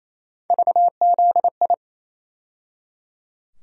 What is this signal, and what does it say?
Morse code, heard as a keyed tone, about 28 wpm, 700 Hz. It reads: VZI
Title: USA Call Sign Postfixes - 3 Letters 28wpm